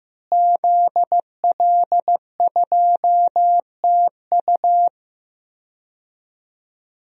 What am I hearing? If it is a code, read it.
Morse code, heard as a keyed tone, about 16 wpm, 700 Hz. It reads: ZL2TU
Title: DXCC-100 Call Signs - 5 Characters - Repeated 3x - 15wpm